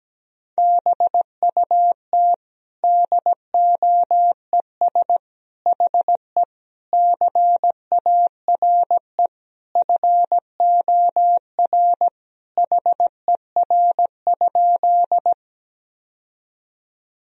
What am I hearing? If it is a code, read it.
Morse code, heard as a keyed tone, about 17 wpm, 700 Hz. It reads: BUT DOES HE CARE FOR HER?
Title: Sentences from Top 1000 Words - Review Part 1 - 17wpm